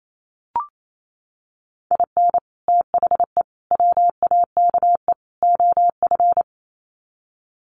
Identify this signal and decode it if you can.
Morse code, heard as a keyed tone, about 28 wpm, 700 Hz. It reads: IN THE WAKE OF
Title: Sets of 4 Words 28wpm